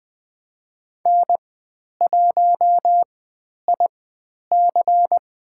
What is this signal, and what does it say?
Morse code, heard as a keyed tone, about 20 wpm, spaced as for 10 wpm, 700 Hz. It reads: N1IC
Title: 109 Lesson 18g - Letter m - Call Signs 20wpm